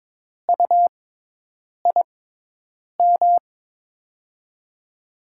Morse code, heard as a keyed tone, about 22 wpm, spaced as for 8 wpm, 700 Hz. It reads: UIM
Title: USA Call Signs - 6x Character Spacing - 22wpm